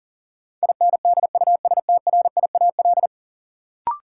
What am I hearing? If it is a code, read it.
Morse code, heard as a keyed tone, about 40 wpm, 700 Hz. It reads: INDUSTRIAL
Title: Sets of 5 Words - New Word Drill - 40wpm